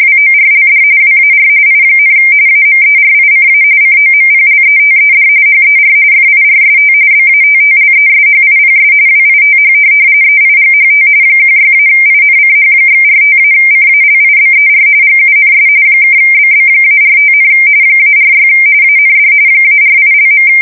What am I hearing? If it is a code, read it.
RTTY, about 45.45 baud. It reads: WORLD WAR I, AMERICA COMPLETED HER FIRST BATTLESHIPS WITH EIGHT 16-INCH/45-CALIBER GUNS (BB-45, -46 AND -48).
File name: fsk170 / USS-New-Jersey